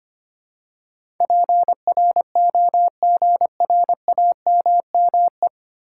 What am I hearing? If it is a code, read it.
Morse code, heard as a keyed tone, about 25 wpm, 700 Hz. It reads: PROGRAMME
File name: 9 Letter Words - First 3 Letters Given - 25wpm